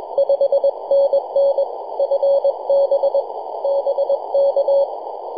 Morse code, heard as a keyed tone, about 22 wpm, 560 Hz, with noise. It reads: 5NN FB BK
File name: yj0nk-18-075mhz-cw